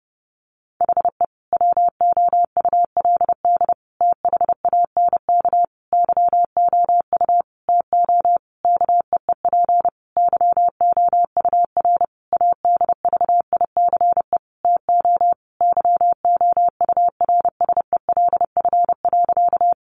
Morse code, heard as a keyed tone, about 30 wpm, 700 Hz. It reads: HE WOULD THANK YOU TO KEEP YOUR ADVICE TO YOURSELF.
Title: Sentences from Top 1100 Words 30wpm